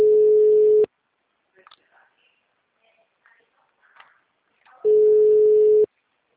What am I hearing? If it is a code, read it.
Ringback tone (Europe, 425 Hz).
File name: out- -2043- - - .18